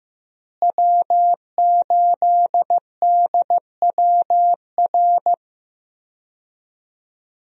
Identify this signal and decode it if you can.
Morse code, heard as a keyed tone, about 15 wpm, 700 Hz. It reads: W8DWR
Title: USA Call Signs - 5 Characters - Repeated 3x - 15wpm